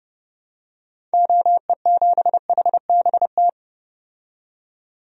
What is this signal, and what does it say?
Morse code, heard as a keyed tone, about 30 wpm, 700 Hz. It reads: OE7HBT
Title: DXCC-100 Call Signs - 6 Characters - 30wpm